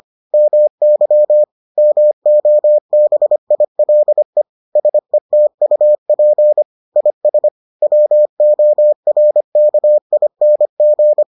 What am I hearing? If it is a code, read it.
Morse code, heard as a keyed tone, about 25 wpm, 600 Hz. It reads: MY MOBILE SETUP IS WORKING